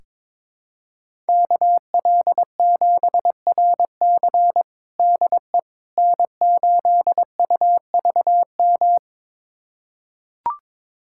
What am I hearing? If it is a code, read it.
Morse code, heard as a keyed tone, about 22 wpm, 700 Hz. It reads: KL7RC DE N8UVM
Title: USA Callsign DE Callsign - 5 Characters - 22wpm